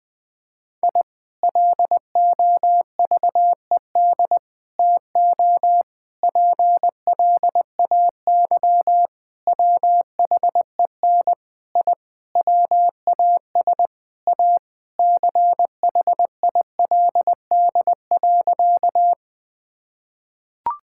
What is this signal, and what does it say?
Morse code, heard as a keyed tone, about 20 wpm, 700 Hz. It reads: I LOVED TO PLAY WHEN I WAS A CHILD.
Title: Sentences with 4-Word Phrases - 20wpm